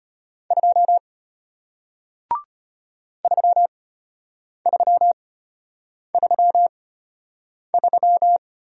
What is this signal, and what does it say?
Morse code, heard as a keyed tone, about 33 wpm, spaced as for 8 wpm, 700 Hz. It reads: 2 3333